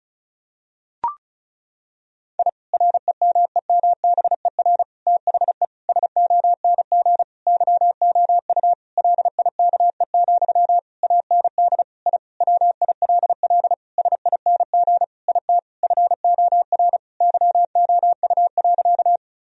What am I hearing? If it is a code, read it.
Morse code, heard as a keyed tone, about 36 wpm, 700 Hz. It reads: I REMEMBER THE SONG YOU LIKE, AND I WILL SING IT FOR YOU.
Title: Sentences from Top 1000 Words - Review Part 2 - 35wpm